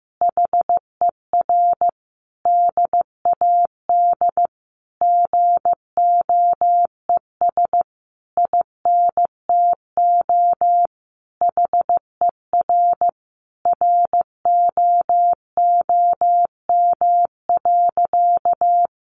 Morse code, heard as a keyed tone, about 15 wpm, 700 Hz. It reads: HER DAD GOES INTO HER ROOM.
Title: Additional Easy Phrases 15wpm